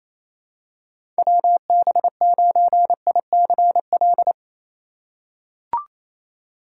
Morse code, heard as a keyed tone, about 28 wpm, 700 Hz. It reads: WB9ICL
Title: USA Call Signs - 6 Characters - 2x Character Spacing - 28wpm